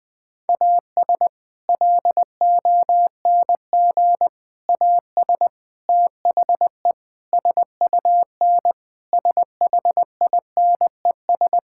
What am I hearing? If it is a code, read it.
Morse code, heard as a keyed tone, about 21 wpm, 700 Hz. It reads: AS LONG AS THE SUN SHINES
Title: Sentences with 4-Word Phrases - Spoken Beginning - 20wpm